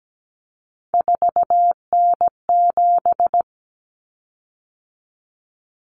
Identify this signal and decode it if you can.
Morse code, heard as a keyed tone, about 17 wpm, 700 Hz. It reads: 4N7
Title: Common Cut Numbers - 3-Digit 17wpm